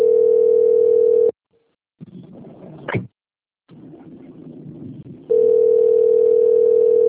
Ringback tone (North America), 440 + 480 Hz.
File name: out- - - - - .13624